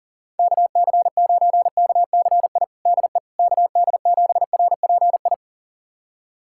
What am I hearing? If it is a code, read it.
Morse code, heard as a keyed tone, about 40 wpm, 700 Hz. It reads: KC9KCI DE KD7RPI
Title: USA Callsign DE Callsign 40wpm